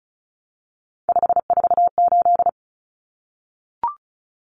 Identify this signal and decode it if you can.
Morse code, heard as a keyed tone, about 34 wpm, 700 Hz. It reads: 548
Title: Numbers - 3-Digit 35wpm